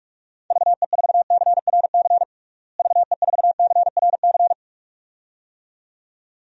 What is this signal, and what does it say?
Morse code, heard as a keyed tone, about 46 wpm, 700 Hz. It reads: VE4CRC VE4CRC
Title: DXCC-100 Call Signs - 6 Characters - Repeated 2x - 45wpm